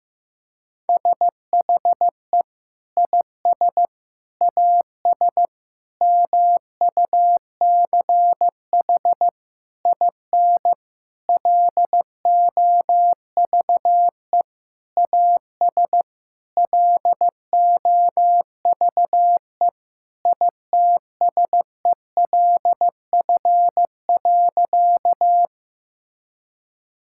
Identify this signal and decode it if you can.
Morse code, heard as a keyed tone, about 15 wpm, 700 Hz. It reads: SHE IS AS MUCH IN LOVE AS LOVE ITSELF.
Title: Sentences from Top 1000 Words - Review Part 5 - 15wpm